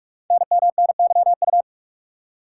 Morse code, heard as a keyed tone, about 44 wpm, 700 Hz. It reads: NMNYU
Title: Code Groups - 5 Letters - 45wpm